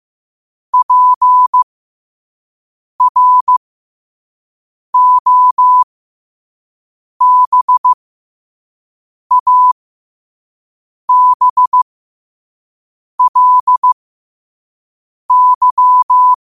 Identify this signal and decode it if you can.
Morse code, heard as a keyed tone, about 15 wpm, spaced as for 5 wpm, 1000 Hz. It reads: PROBABLY
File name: qotdfiles / 5wpm